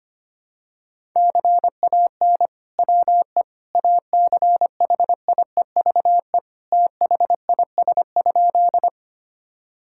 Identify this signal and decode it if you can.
Morse code, heard as a keyed tone, about 25 wpm, 700 Hz. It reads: CAN WE ACHIEVE THIS?
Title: Sentences from Top 1000 Words - Review Part 6 - 25wpm